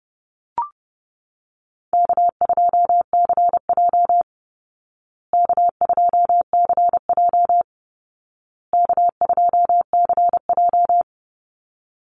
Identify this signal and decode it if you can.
Morse code, heard as a keyed tone, about 30 wpm, 700 Hz. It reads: K2CJ K2CJ K2CJ
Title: Easy Call Signs Repeated 3x - 30wpm